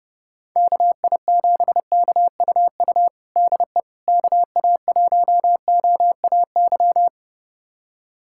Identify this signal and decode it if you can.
Morse code, heard as a keyed tone, about 32 wpm, 700 Hz. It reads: KI7KUU DE KA1OAY
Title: USA Callsign DE Callsign 30wpm